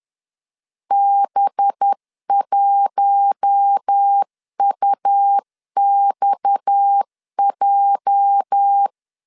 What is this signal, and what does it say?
Morse code, heard as a keyed tone, about 10 wpm, 800 Hz. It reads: B1UXJ
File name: LEZIONI / 9 286 10wpm 2735